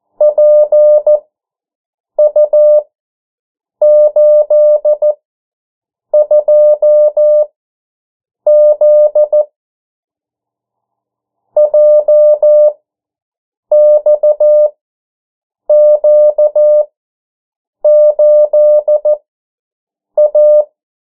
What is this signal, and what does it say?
Morse code, heard as a keyed tone, about 14 wpm, spaced as for 7 wpm, 600 Hz. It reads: PU82Z JXQ8A
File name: morse mp3 / 09 urok 8